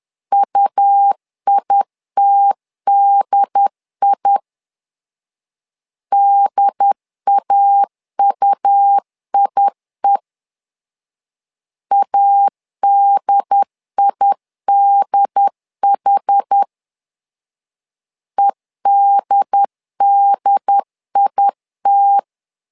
Morse code, heard as a keyed tone, about 10 wpm, 800 Hz. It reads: UITDI DAUIE ADIDH EDDIT
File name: LEZIONI / 2 UHDN / 2 UHD 10wpm 1826